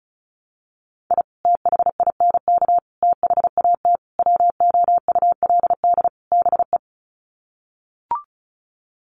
Morse code, heard as a keyed tone, about 34 wpm, 700 Hz. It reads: I THINK THAT WOULD BE
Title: Sets of 5 Words Encore2 35wpm